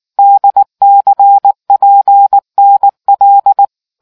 Morse code, heard as a keyed tone, about 19 wpm, 800 Hz. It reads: DCPNL